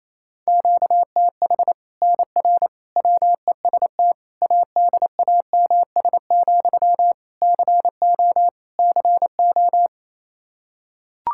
Morse code, heard as a keyed tone, about 28 wpm, 700 Hz. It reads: QTH NR WEST ADAMS, CO CO
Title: Practice QSO - Exchange 1 - Beginning - 28wpm